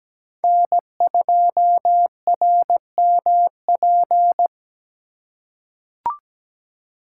Morse code, heard as a keyed tone, about 17 wpm, 700 Hz. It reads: N2RMP
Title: USA Call Signs - 5 Characters - 3x Character Spacing - 17wpm